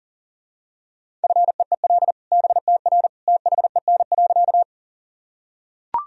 Morse code, heard as a keyed tone, about 40 wpm, 700 Hz. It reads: FEEL BTR THEN.
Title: Sentences from Top 400 Words and Abbreviations 40wpm